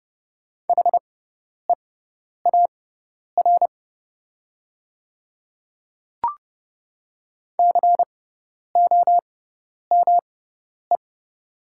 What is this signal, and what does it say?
Morse code, heard as a keyed tone, about 30 wpm, spaced as for 10 wpm, 700 Hz. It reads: HEAR COME
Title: 6x Character Spacing - Top 300 Words - 30wpm